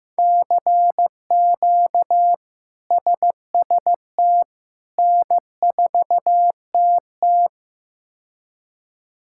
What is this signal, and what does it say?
Morse code, heard as a keyed tone, about 15 wpm, 700 Hz. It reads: CQ SST N4TT
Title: K1USN SST QSO Practice - 15wpm